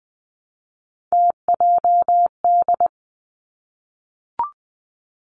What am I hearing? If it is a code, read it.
Morse code, heard as a keyed tone, about 20 wpm, 700 Hz. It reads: TJD
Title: Three Challenging Letter-Number Combinations 20wpm